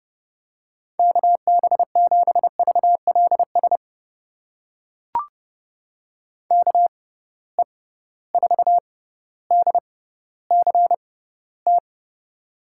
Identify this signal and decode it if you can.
Morse code, heard as a keyed tone, about 32 wpm, spaced as for 16 wpm, 700 Hz. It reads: KB7VLS KE4DCT
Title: USA Call Signs - 6 Characters - 6x Character Spacing - 30wpm